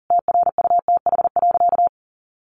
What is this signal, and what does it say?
Morse code, heard as a keyed tone, about 40 wpm, 700 Hz. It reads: TRUTH.
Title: Sentences from Top 1100 Words 40wpm